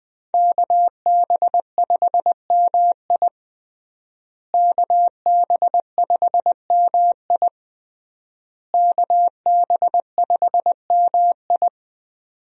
Morse code, heard as a keyed tone, about 20 wpm, 700 Hz. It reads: KB5MI KB5MI KB5MI
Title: USA Call Signs - 5 Characters - Repeated 3x - 20wpm